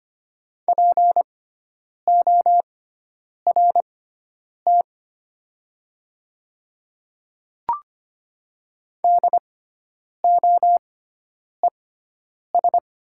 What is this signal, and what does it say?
Morse code, heard as a keyed tone, about 25 wpm, spaced as for 9 wpm, 700 Hz. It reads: PORT DOES
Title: 6x Character Spacing - Top 200 Words - 25wpm